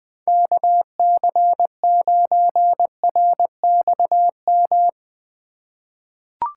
Morse code, heard as a keyed tone, about 20 wpm, 700 Hz. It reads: KC9RXM
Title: USA Call Signs - 6 Characters - 4x Character Spacing - 20wpm